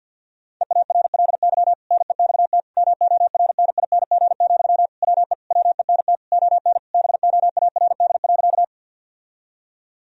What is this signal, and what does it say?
Morse code, heard as a keyed tone, about 50 wpm, 700 Hz. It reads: EARLY NEXT MORNING, WE WENT ON BOARD.
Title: Sentences from Top 1000 Words - Review Part 4 - 50wpm